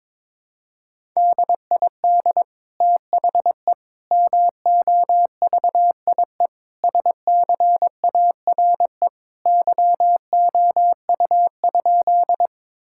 Morse code, heard as a keyed tone, about 21 wpm, 700 Hz. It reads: DID THE MOVIE SCARE YOU?
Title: Sentences from Top 1000 Words - Review Part 5 - 22wpm